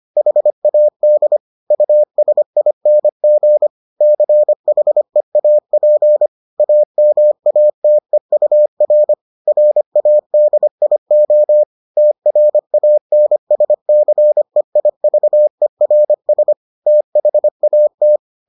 Morse code, heard as a keyed tone, about 25 wpm, 600 Hz. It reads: HAD USING CHEAP AMATEUR RADIO TRANSCEIVERS THAT